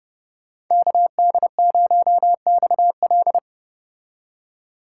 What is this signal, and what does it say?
Morse code, heard as a keyed tone, about 35 wpm, 700 Hz. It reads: KD0XL